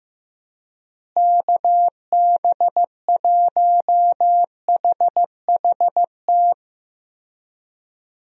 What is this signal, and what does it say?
Morse code, heard as a keyed tone, about 15 wpm, 700 Hz. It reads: KB1HHT
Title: USA Call Signs - 6 Characters - Repeated 3x - 15wpm